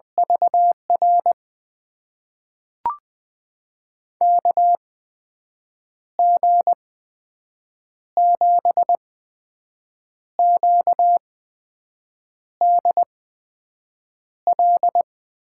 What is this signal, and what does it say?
Morse code, heard as a keyed tone, about 20 wpm, spaced as for 6 wpm, 700 Hz. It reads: VR KG7QDL
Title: USA Call Signs - 6 Characters - 8x Character Spacing - 20wpm